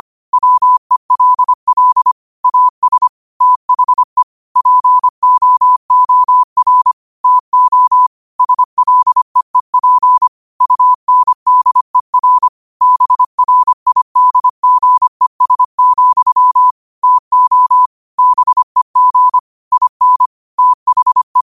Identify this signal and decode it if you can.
Morse code, heard as a keyed tone, about 25 wpm, 1000 Hz. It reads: WELL AS THE POOR TO SLEEP UNDER BRIDGES, TO BEG IN THE
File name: qotdfiles / 25wpm